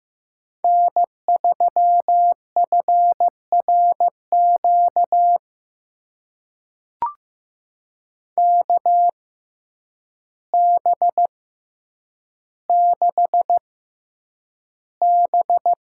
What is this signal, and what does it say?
Morse code, heard as a keyed tone, about 15 wpm, spaced as for 8 wpm, 700 Hz. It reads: N3FRQ KB6B
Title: USA Call Signs - 6x Character Spacing - 15wpm